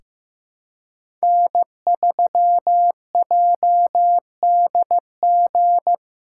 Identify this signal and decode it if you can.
Morse code, heard as a keyed tone, about 16 wpm, 700 Hz. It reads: N3JDG
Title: USA Callsign DE Callsign 15wpm